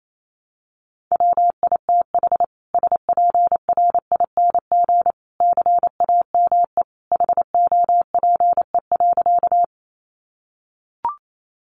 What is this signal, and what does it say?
Morse code, heard as a keyed tone, about 28 wpm, 700 Hz. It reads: WITH SPRING CAME HOPE.
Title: Sentences from Top 1200 Words 28wpm